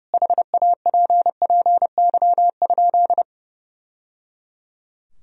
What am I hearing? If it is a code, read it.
Morse code, heard as a keyed tone, about 30 wpm, 700 Hz. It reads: HAPPY?
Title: Sentences from Top 1000 Words - Review Part 2 - 30wpm